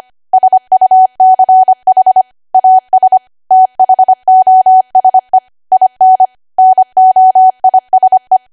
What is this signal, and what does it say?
Morse code, heard as a keyed tone, about 25 wpm, 750 Hz. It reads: SUCH AS THOSE IN NOISE
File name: cwa / W1AW-Level3 / 25WPM